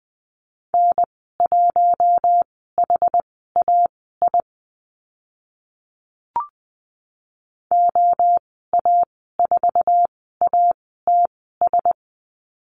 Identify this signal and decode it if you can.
Morse code, heard as a keyed tone, about 20 wpm, spaced as for 14 wpm, 700 Hz. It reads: N1HAI OA4ATS
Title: 069 Lesson 12g - Letter d - Call Signs 20wpm